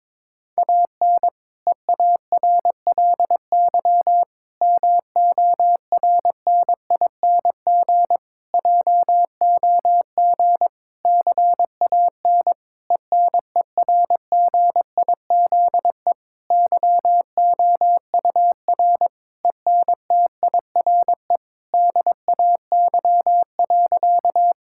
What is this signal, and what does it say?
Morse code, heard as a keyed tone, about 22 wpm, 700 Hz. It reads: AN EARLY MORNING JOG CAN ENERGIZE YOUR ENTIRE DAY.
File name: Difficult Sentences with Hints - 22wpm